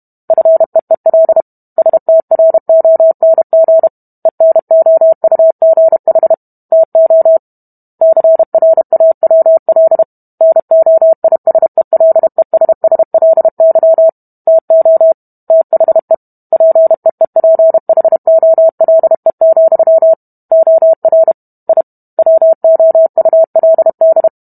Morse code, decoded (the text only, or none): FEEL STRONG ENOUGH TO CRAWL NOISELESSLY TO THE PEEPHOLE, OR I WOULD